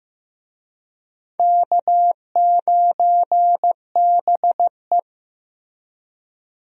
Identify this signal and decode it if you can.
Morse code, heard as a keyed tone, about 15 wpm, 700 Hz. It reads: K9BE
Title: USA Call Signs - 4 Characters - Repeated 3x - 15wpm